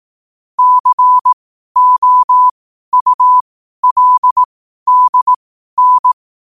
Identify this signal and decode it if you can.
Morse code, heard as a keyed tone, about 18 wpm, spaced as for 12 wpm, 1000 Hz. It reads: COULDN
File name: qotdfiles / 12wpm